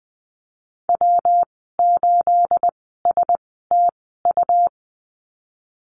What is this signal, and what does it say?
Morse code, heard as a keyed tone, about 20 wpm, spaced as for 14 wpm, 700 Hz. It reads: W8STU
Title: 245 Lesson 39g - Letter z - Call Signs 20wpm